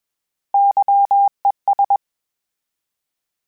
Morse code, heard as a keyed tone, about 21 wpm, 800 Hz. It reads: YES
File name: Misc / Misc-mp3s / Misc @20